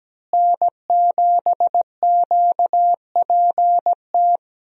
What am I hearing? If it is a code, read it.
Morse code, heard as a keyed tone, about 17 wpm, 700 Hz. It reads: N7QPT
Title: USA Call Signs - 5 Characters - Repeated 3x - 17wpm